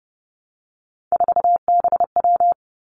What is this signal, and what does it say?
Morse code, heard as a keyed tone, about 30 wpm, 700 Hz. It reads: 4BW
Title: Three Challenging Letter-Number Combinations 30wpm